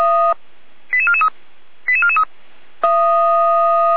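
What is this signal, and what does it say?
Five-tone selective call (ZVEI), heard as quick runs of five tones, tones of about 70 ms. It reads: 88443 88443
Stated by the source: ZVEI five-tone selective call 88497 received by SDR, from the Feuerwehr Orlamünde audio server.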